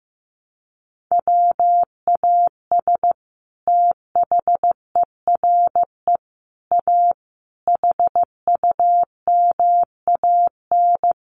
Morse code, heard as a keyed tone, about 15 wpm, 700 Hz. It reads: WAS THERE A HUMAN
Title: Sentences from Top 1300 Words 15wpm